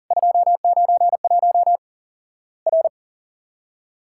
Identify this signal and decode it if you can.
Morse code, heard as a keyed tone, about 40 wpm, 700 Hz. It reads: 291 R
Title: Worked All Europe DX Contest - QTC Exchange - 40wpm